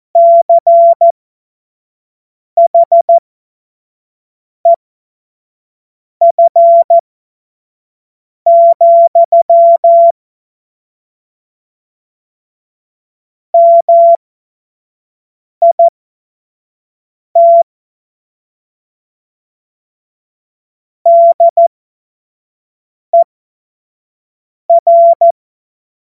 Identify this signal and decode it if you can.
Morse code, heard as a keyed tone, about 14 wpm, spaced as for 5 wpm, 680 Hz. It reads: CHEF, MIT DER
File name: Morsekurs / Witze / Witz 31. Januar 2023 25 F70